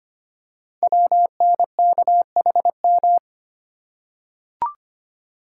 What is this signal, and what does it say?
Morse code, heard as a keyed tone, about 25 wpm, 700 Hz. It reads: WNKHM
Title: Code Groups - 5 Letters - 25wpm